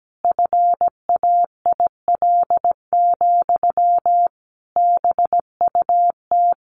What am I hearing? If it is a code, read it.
Morse code, heard as a keyed tone, about 17 wpm, 700 Hz. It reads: FAIL, BUT
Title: Sentences from Top 1100 Words 17wpm